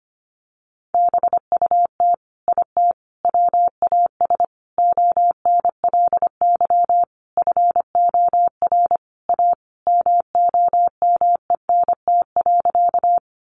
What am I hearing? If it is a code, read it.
Morse code, heard as a keyed tone, about 25 wpm, 700 Hz. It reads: BUT IT WAS ONLY FOR A MOMENT.
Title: Sentences from Top 1000 Words 25wpm